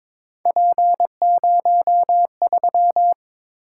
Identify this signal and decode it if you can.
Morse code, heard as a keyed tone, about 22 wpm, 700 Hz. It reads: P03
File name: practice / 22wpm / Three Letter-Number Combinations - 22wpm